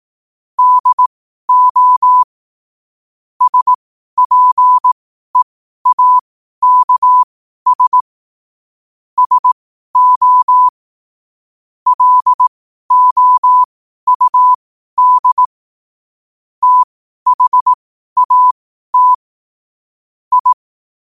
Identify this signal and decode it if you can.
Morse code, heard as a keyed tone, about 18 wpm, spaced as for 12 wpm, 1000 Hz. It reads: DO SPEAKS SO LOUD THAT I
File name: qotdfiles / 12wpm